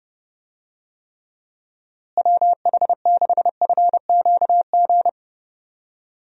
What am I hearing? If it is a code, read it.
Morse code, heard as a keyed tone, about 30 wpm, 700 Hz. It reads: WH6FQG
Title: USA Call Signs - 6 Characters - Repeated 3x - 30wpm